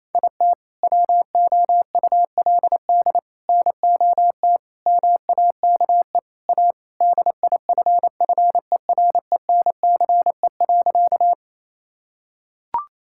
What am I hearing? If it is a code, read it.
Morse code, heard as a keyed tone, about 28 wpm, 700 Hz. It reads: IT WOULD NOT MAKE A DIFFERENCE.
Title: Sentences from Top 1000 Words - Review Part 5 - 28wpm